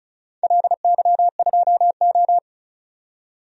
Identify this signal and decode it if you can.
Morse code, heard as a keyed tone, about 36 wpm, 700 Hz. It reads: LY2O
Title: DXCC-100 Call Signs - 4 Characters - 35wpm